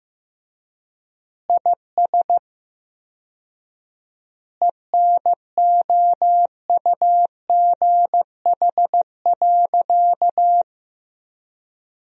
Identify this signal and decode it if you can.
Morse code, heard as a keyed tone, about 16 wpm, 700 Hz. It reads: IS ENOUGH.
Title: 4x Word-Spacing - Sentences from Top 400 Words - 15wpm